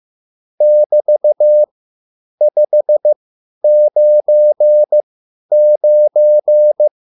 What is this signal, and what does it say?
Morse code, heard as a keyed tone, about 15 wpm, spaced as for 10 wpm, 600 Hz. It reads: <BT>599
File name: QSL / CW lesson